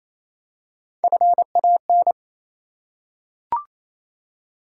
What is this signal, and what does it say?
Morse code, heard as a keyed tone, about 27 wpm, 700 Hz. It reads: FAN
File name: Easy Words - Speed-Racing with Hints - 22wpm